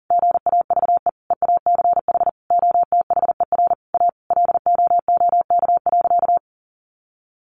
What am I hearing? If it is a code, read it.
Morse code, heard as a keyed tone, about 40 wpm, 700 Hz. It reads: GAVE EACH OTHER A LOOK.
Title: A Sound Of Thunder 40wpm